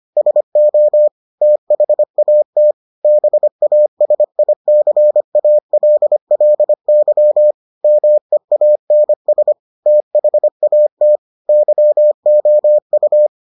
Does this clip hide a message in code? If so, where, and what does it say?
Yes: Morse code from 0.2 to 13.3 s, heard as a keyed tone, about 25 wpm, 600 Hz. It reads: SO THAT BASICALLY MEANS THAT YOU